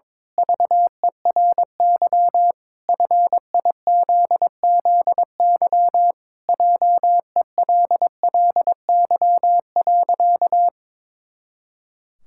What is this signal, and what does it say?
Morse code, heard as a keyed tone, about 22 wpm, 700 Hz. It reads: VERY FIZZY JELLY.
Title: Pangram Challenge v2 - 22wpm